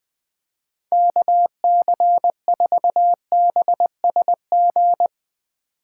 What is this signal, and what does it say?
Morse code, heard as a keyed tone, about 21 wpm, 700 Hz. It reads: KC4BSG
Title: USA Call Signs - 6 Characters - 8x Character Spacing - 20wpm